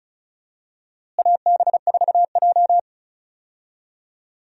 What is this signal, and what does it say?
Morse code, heard as a keyed tone, about 36 wpm, 700 Hz. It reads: AB4J
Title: USA Call Signs - 4 Characters - Repeated 2x - 35wpm